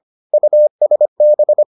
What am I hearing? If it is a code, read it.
Morse code, heard as a keyed tone, about 25 wpm, 600 Hz. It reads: USB